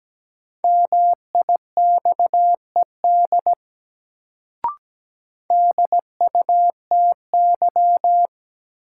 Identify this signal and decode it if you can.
Morse code, heard as a keyed tone, about 17 wpm, 700 Hz. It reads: MIXED DUTY
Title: Sentences from Top 1500 Words - New Word Drill - 17wpm